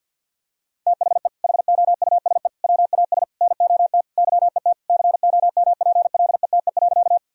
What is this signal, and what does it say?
Morse code, heard as a keyed tone, about 50 wpm, 700 Hz. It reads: THE HOUSE WAS NOT YET COMPLETE.
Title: Sentences from Top 1000 Words - Review Part 2 - 50wpm